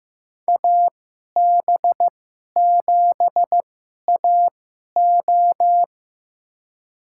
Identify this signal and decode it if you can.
Morse code, heard as a keyed tone, about 15 wpm, spaced as for 11 wpm, 700 Hz. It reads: AB7AO
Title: USA Call Signs - 5 Characters - 2x Character Spacing - 15wpm